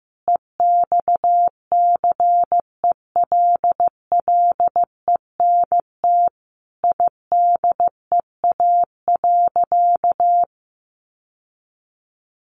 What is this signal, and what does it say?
Morse code, heard as a keyed tone, about 15 wpm, 700 Hz. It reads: EXCELLENT IDEA.
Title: Sentences from Top 1000 Words 15wpm